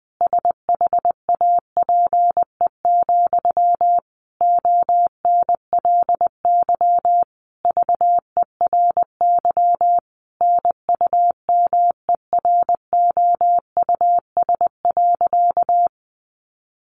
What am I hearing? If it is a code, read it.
Morse code, heard as a keyed tone, about 19 wpm, 700 Hz. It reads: SHAPE, ONLY VERY NUMEROUS.
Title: Sentences from Top 1400 Words 20wpm